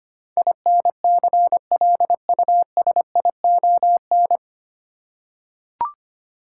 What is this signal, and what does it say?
Morse code, heard as a keyed tone, about 25 wpm, 700 Hz. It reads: INCLUSION
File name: Nine Letter Words - 25wpm